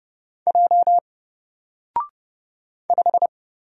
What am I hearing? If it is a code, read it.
Morse code, heard as a keyed tone, about 30 wpm, 700 Hz. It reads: J 5
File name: Single-Letter-Number-30wpm